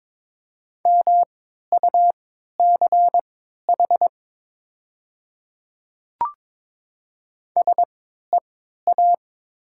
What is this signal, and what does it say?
Morse code, heard as a keyed tone, about 22 wpm, spaced as for 12 wpm, 700 Hz. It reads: MUCH SEA
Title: 3x Character Spacing - Top 300 Words - 22wpm